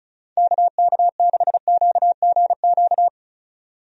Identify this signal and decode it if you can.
Morse code, heard as a keyed tone, about 34 wpm, 700 Hz. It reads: KK6QGQ
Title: USA Call Signs - 6 Characters - 35wpm